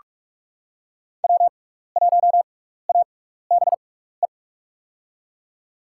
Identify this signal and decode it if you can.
Morse code, heard as a keyed tone, about 43 wpm, spaced as for 15 wpm, 700 Hz. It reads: W1ABE
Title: USA Call Signs - 6x Character Spacing - 45wpm